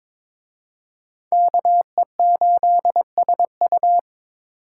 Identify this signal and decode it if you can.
Morse code, heard as a keyed tone, about 26 wpm, 690 Hz. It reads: KE8SU
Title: Speed Racing - USA Call Signs 22wpm